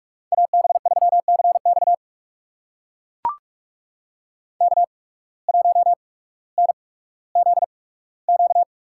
Morse code, heard as a keyed tone, about 41 wpm, spaced as for 19 wpm, 700 Hz. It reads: AB3CX K1NZQ
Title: USA Call Signs - 5 Characters - 8x Character Spacing - 45wpm